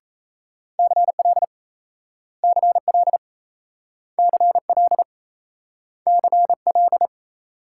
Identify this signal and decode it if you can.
Morse code, heard as a keyed tone, about 38 wpm, 700 Hz. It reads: CL CL CL CL